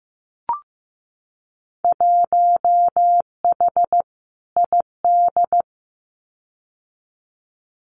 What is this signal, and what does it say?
Morse code, heard as a keyed tone, about 15 wpm, 700 Hz. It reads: 1H ID
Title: Winter Field Day Exchanges 15wpm